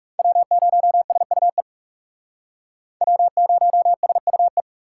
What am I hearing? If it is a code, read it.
Morse code, heard as a keyed tone, about 37 wpm, 700 Hz. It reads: W0SUE W0SUE
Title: Speed Racing - USA 1x3 Call Signs 30wpm